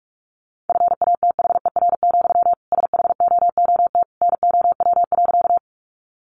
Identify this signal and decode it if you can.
Morse code, heard as a keyed tone, about 44 wpm, 700 Hz. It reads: FATHER, SHOOT NOW.
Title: Sentences from Top 1200 Words 45wpm